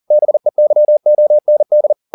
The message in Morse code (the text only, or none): BEYOND